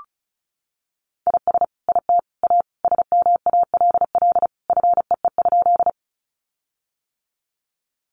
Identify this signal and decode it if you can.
Morse code, heard as a keyed tone, about 34 wpm, 700 Hz. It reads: IS IT A SMALL FEE?
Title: Sentences from Top 1500 Words 35wpm